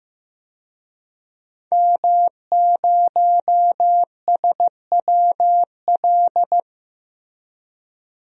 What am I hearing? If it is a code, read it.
Morse code, heard as a keyed tone, about 15 wpm, 700 Hz. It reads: M0SWL
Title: DXCC-100 Call Signs - 5 Characters - Repeated 3x - 15wpm